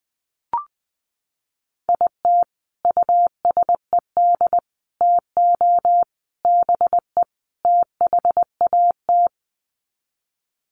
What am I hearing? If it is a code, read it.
Morse code, heard as a keyed tone, about 20 wpm, 700 Hz. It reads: IT USED TO BE THAT
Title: Sets of 5 Words Encore2 20wpm